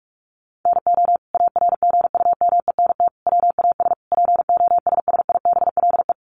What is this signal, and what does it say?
Morse code, heard as a keyed tone, about 44 wpm, 700 Hz. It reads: NO ARGUMENT WAS POSSIBLE
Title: Sentences from Top 1400 Words 45wpm